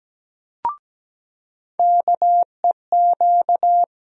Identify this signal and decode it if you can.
Morse code, heard as a keyed tone, about 17 wpm, 700 Hz. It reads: KEQ
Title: USA Call Sign Postfixes - 3 Letters 17wpm